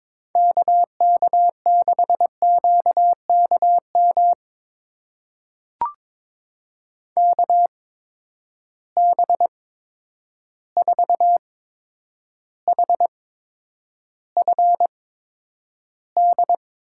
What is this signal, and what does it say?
Morse code, heard as a keyed tone, about 22 wpm, spaced as for 9 wpm, 700 Hz. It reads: KK6QKM KB4HFD
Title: USA Call Signs - 6 Characters - 8x Character Spacing - 22wpm